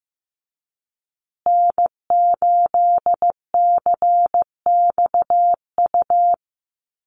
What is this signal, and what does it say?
Morse code, heard as a keyed tone, about 15 wpm, 700 Hz. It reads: N8CXU
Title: USA Call Signs Repeated 2x 15wpm